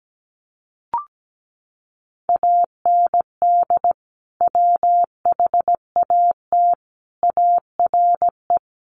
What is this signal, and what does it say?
Morse code, heard as a keyed tone, about 17 wpm, 700 Hz. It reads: AND WHAT ARE
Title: Sentences from Top 800 Words 17wpm